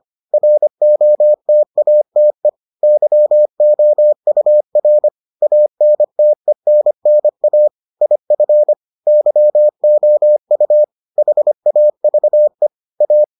Morse code, heard as a keyed tone, about 25 wpm, 600 Hz. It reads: ROTATE YOUR ANTENNA IF YOU HAVE A